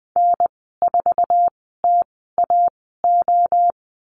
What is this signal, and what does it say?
Morse code, heard as a keyed tone, about 20 wpm, spaced as for 14 wpm, 700 Hz. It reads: N4TAO
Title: 055 Lesson 10g - Letter r - Call Signs 20wpm